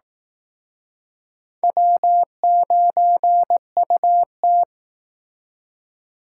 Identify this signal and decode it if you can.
Morse code, heard as a keyed tone, about 18 wpm, 700 Hz. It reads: W9UT
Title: Speed Racing - USA Call Signs 15wpm